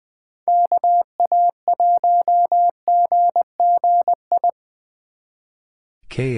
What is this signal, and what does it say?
Morse code, heard as a keyed tone, about 22 wpm, 700 Hz. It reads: KA1GGI
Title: Speed Racing - USA 2x3 Call Signs 20wpm